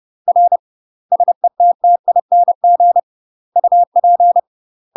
Morse code, heard as a keyed tone, about 30 wpm, 700 Hz. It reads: R SETTING UP
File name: UK MASTER QSOs TRAINING / 30WPM / UK MASTER QSOs 0042